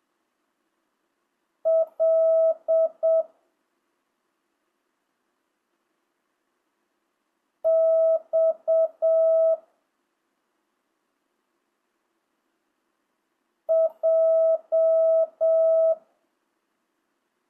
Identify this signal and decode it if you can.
Morse code, heard as a keyed tone, about 7 wpm, 650 Hz. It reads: L X J
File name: jh8cbh / 2025cw